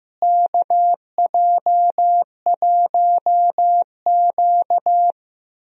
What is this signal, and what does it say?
Morse code, heard as a keyed tone, about 16 wpm, 700 Hz. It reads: KJ1Q
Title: USA Call Signs - 4 Characters - Repeated 2x - 15wpm